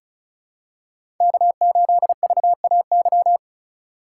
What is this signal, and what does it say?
Morse code, heard as a keyed tone, about 36 wpm, 700 Hz. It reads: K8VAY
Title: USA Call Signs - 5 Characters - 2x Character Spacing - 35wpm